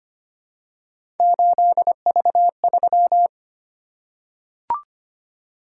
Morse code, heard as a keyed tone, about 25 wpm, 700 Hz. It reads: 8V3
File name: Three Letter-Number Combinations - 25wpm